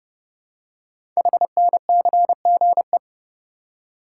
Morse code, heard as a keyed tone, about 30 wpm, 700 Hz. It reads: HNCGE
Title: Code Groups - 5 Letters - 30wpm